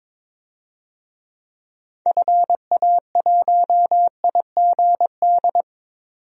Morse code, heard as a keyed tone, about 22 wpm, 700 Hz. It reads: FA1IGD
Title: DXCC-100 Call Signs - 6 Characters - 22wpm